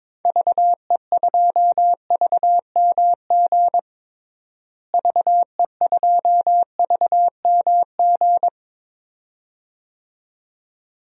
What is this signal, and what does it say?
Morse code, heard as a keyed tone, about 22 wpm, 700 Hz. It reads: VE2VMG VE2VMG
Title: DXCC-100 Call Signs - 6 Characters - Repeated 2x - 22wpm